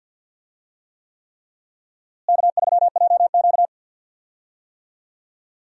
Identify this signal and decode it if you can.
Morse code, heard as a keyed tone, about 51 wpm, 700 Hz. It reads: K3JQ
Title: USA Call Signs - 4 Characters - Repeated 3x - 50wpm